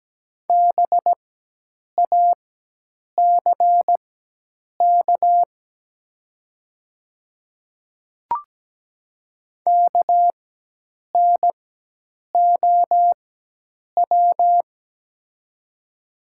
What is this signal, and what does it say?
Morse code, heard as a keyed tone, about 17 wpm, spaced as for 8 wpm, 700 Hz. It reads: BACK KNOW
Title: 4x Character Spacing - Top 200 Words - 17wpm